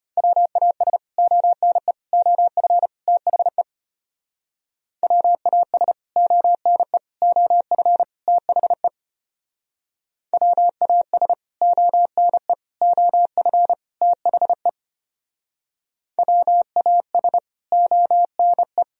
Morse code, heard as a keyed tone, about 30 wpm, 700 Hz. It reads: WAS ONE OF THE WAS ONE OF THE WAS ONE OF THE WAS ONE
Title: Speed-Racing - Sets of 4 Words 25wpm